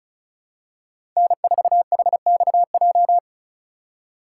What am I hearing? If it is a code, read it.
Morse code, heard as a keyed tone, about 36 wpm, 700 Hz. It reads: N4HXJ
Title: USA Call Signs - 2x Character Spacing - 35wpm